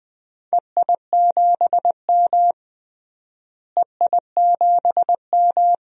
Morse code, heard as a keyed tone, about 20 wpm, 690 Hz. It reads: EI7M EI7M
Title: DXCC-100 Call Signs - 4 Characters - Repeated 3x - 20wpm